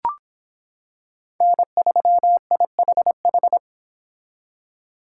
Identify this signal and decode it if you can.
Morse code, heard as a keyed tone, about 25 wpm, 700 Hz. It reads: N3IHH